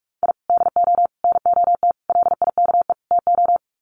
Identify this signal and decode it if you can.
Morse code, heard as a keyed tone, about 44 wpm, 700 Hz. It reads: I DO NOT LIKE TO
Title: Sets of 5 Words 45wpm